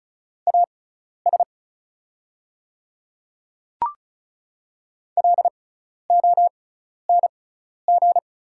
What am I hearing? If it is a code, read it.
Morse code, heard as a keyed tone, about 34 wpm, spaced as for 12 wpm, 700 Hz. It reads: AS LONG
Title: 6x Character Spacing - Top 100 Words - 35wpm